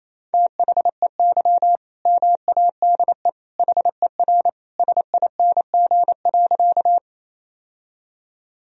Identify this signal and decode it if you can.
Morse code, heard as a keyed tone, about 28 wpm, 700 Hz. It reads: THEY MADE HER SING.
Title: Sentences from Top 1000 Words - Review Part 2 - 28wpm